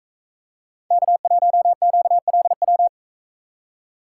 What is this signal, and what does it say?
Morse code, heard as a keyed tone, about 42 wpm, 700 Hz. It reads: K1QLW